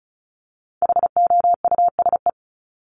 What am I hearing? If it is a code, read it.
Morse code, heard as a keyed tone, about 34 wpm, 700 Hz. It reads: HOUSE
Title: Top 200 Words Repeated 35wpm